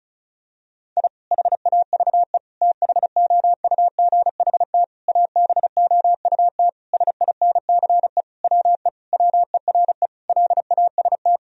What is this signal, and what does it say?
Morse code, heard as a keyed tone, about 36 wpm, 700 Hz. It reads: I HAVE THOUGHT ABOUT SINCE WE WERE LAST
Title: Sentences from Top 1000 Words - Review Part 1 - 35wpm